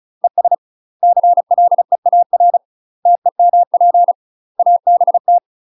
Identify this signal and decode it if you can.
Morse code, heard as a keyed tone, about 34 wpm, 700 Hz. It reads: ES CLEAR TEMP ABT
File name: UK MASTER QSOs TRAINING / 35WPM / UK MASTER QSOs 0026